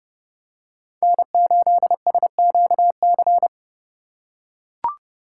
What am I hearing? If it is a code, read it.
Morse code, heard as a keyed tone, about 30 wpm, 700 Hz. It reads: N8SQC
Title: USA Call Signs - 5 Characters - 3x Character Spacing - 30wpm